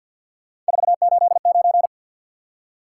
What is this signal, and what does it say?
Morse code, heard as a keyed tone, about 50 wpm, 700 Hz. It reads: 489
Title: Top 180 QSO Elements 50wpm